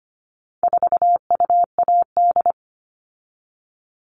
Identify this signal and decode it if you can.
Morse code, heard as a keyed tone, about 25 wpm, 700 Hz. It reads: 4UAD
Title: All Cut Numbers - 4-Digit 25wpm